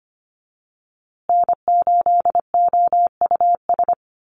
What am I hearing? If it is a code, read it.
Morse code, heard as a keyed tone, about 25 wpm, 700 Hz. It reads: N8OUS
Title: USA Call Signs Repeated 3x 25wpm